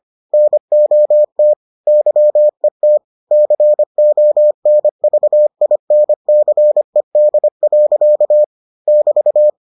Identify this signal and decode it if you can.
Morse code, heard as a keyed tone, about 25 wpm, 600 Hz. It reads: NOT YET CONVINCED. <BT>